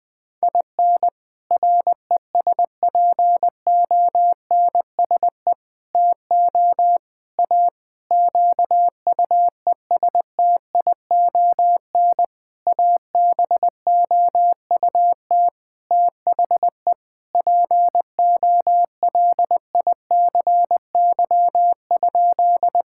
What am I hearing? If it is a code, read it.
Morse code, heard as a keyed tone, about 21 wpm, 700 Hz. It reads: IN RESPONSE TO A QUESTION ABOUT THE POLICY?
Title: Sentences with 6-Word Phrases - Spoken Beginning - 20wpm